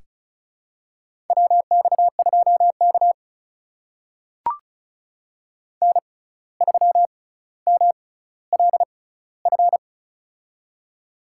Morse code, heard as a keyed tone, about 36 wpm, spaced as for 17 wpm, 700 Hz. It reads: WX2K N3MLF
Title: USA Call Signs - 6x Character Spacing - 35wpm